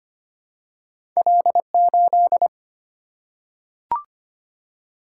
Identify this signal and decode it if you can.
Morse code, heard as a keyed tone, about 25 wpm, 700 Hz. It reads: L8
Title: Single-Letter Single-Number 25wpm